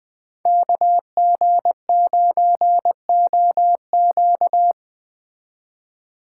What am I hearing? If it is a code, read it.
Morse code, heard as a keyed tone, about 20 wpm, 700 Hz. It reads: KG9OQ